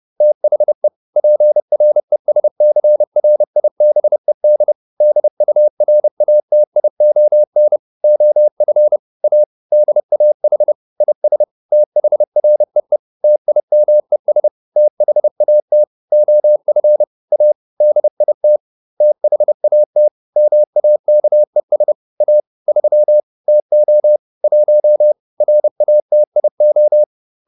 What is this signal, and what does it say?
Morse code, heard as a keyed tone, about 30 wpm, 600 Hz. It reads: THE PRESCRIBED DURATION OF A DAH IS THREE TIMES THAT OF A DIT THAT MAKES A 3 TO 1 RATIO